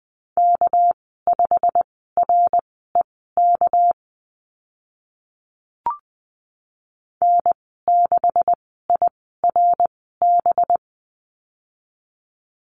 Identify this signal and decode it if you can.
Morse code, heard as a keyed tone, about 20 wpm, spaced as for 14 wpm, 700 Hz. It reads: K5REK N6IRB
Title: 245 Lesson 39g - Letter z - Call Signs 20wpm